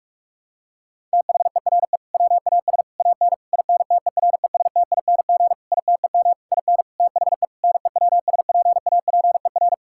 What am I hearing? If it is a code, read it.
Morse code, heard as a keyed tone, about 46 wpm, 700 Hz. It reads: THERE WAS AN INTERESTING ITEM IN THE NEWSPAPER
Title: Sentences from Top 1000 Words - Review Part 6 - 45wpm